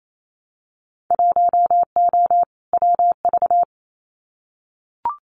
Morse code, heard as a keyed tone, about 28 wpm, 700 Hz. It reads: 1O WV
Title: Winter Field Day Exchanges 28wpm